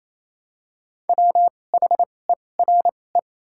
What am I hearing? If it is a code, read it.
Morse code, heard as a keyed tone, about 28 wpm, 700 Hz. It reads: W H E R E
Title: 2x Character Spacing - Top 200 Words - 28wpm